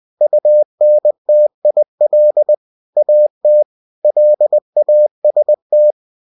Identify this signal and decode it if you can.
Morse code, heard as a keyed tone, about 20 wpm, 600 Hz. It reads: UNTIL AT LAST
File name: winnie-20wpm